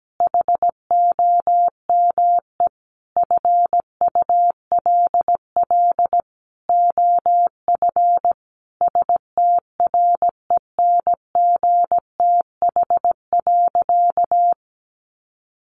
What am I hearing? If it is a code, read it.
Morse code, heard as a keyed tone, about 17 wpm, 700 Hz. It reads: HOME FULL OF STRENGTH.
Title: Sentences from Top 1300 Words 17wpm